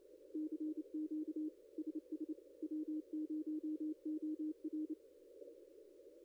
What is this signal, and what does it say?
Morse code, heard as a keyed tone, about 29 wpm, 310 Hz. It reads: CQ SS W0OR
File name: contests / ss-cw-k5tr-op-n5rz -0900